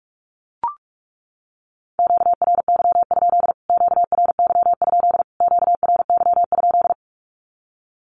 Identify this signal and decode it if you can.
Morse code, heard as a keyed tone, about 44 wpm, 700 Hz. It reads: QRY? QRY? QRY?
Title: Q-Codes Repeated 3x 45wpm